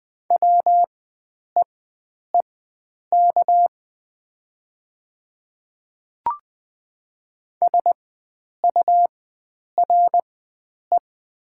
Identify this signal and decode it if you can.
Morse code, heard as a keyed tone, about 20 wpm, spaced as for 9 wpm, 700 Hz. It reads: WEEK SURE
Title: 4x Character Spacing - Top 500 Words - 20wpm